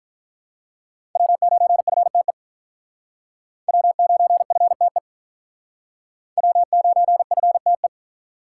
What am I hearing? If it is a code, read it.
Morse code, heard as a keyed tone, about 46 wpm, 700 Hz. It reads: W9FTE W9FTE W9FTE